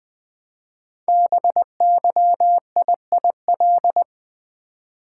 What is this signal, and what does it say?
Morse code, heard as a keyed tone, about 20 wpm, 700 Hz. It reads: BYIIL
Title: Code Groups - 5 Letters - 20wpm